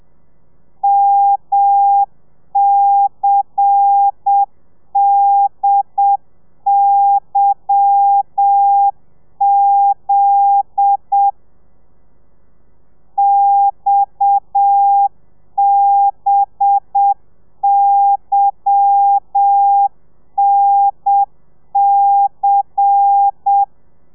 Morse code, heard as a keyed tone, about 7 wpm, 800 Hz. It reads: MCDYZ XBYNC
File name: telegrafia / i2tfj / Lez10 30